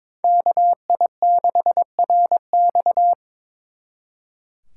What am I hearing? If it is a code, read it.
Morse code, heard as a keyed tone, about 25 wpm, 700 Hz. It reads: KI6RX
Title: Speed Racing - USA 2x2 Call Signs 22wpm